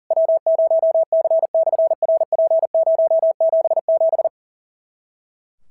Morse code, heard as a keyed tone, about 37 wpm, 650 Hz. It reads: W0C/RP077
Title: SOTA QSO Practice 40wpm